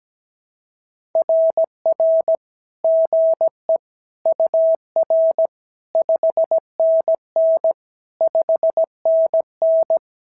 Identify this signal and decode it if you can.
Morse code, heard as a keyed tone, about 17 wpm, 650 Hz. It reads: RR GE UR 5NN 5NN
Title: SOTA QSO Practice 17wpm